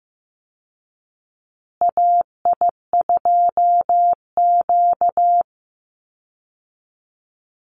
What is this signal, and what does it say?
Morse code, heard as a keyed tone, about 15 wpm, 700 Hz. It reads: AI2Q
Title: Call Signs Repeated 2x - 15wpm